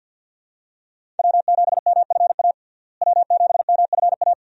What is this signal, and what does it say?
Morse code, heard as a keyed tone, about 52 wpm, 700 Hz. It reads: W7MFA W7MFA
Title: USA Call Signs - 5 Characters - Repeated 2x - 50wpm